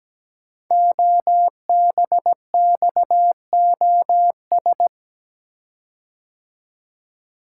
Morse code, heard as a keyed tone, about 17 wpm, 700 Hz. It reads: OBXOS
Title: Code Groups - 5 Letters - 17wpm